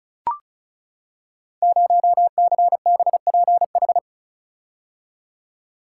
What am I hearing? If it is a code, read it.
Morse code, heard as a keyed tone, about 36 wpm, 690 Hz. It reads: 0CBPH
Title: Code Groups - 5 Letter:Numbers - 35wpm